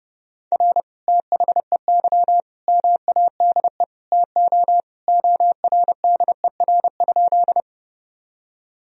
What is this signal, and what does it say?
Morse code, heard as a keyed tone, about 32 wpm, 700 Hz. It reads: R THEY MADE TO ORDER?
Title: Sentences from Top 400 Words and Abbreviations 30wpm